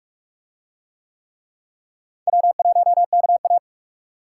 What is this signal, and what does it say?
Morse code, heard as a keyed tone, about 44 wpm, 700 Hz. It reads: W1KA